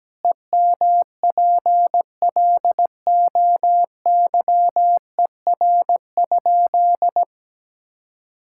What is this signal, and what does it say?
Morse code, heard as a keyed tone, about 17 wpm, 700 Hz. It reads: EMPLOYER?
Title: Sentences from Top 1600 Words 17wpm